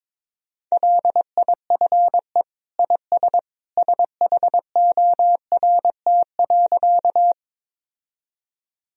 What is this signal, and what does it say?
Morse code, heard as a keyed tone, about 22 wpm, 700 Hz. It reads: LIFE IS SHORT.
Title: Sentences from Top 1000 Words - Review Part 2 - 22wpm